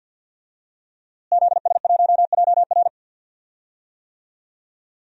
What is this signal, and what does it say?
Morse code, heard as a keyed tone, about 52 wpm, 700 Hz. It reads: ZS1JR
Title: DXCC-100 Call Signs - 5 Characters - Repeated 3x - 50wpm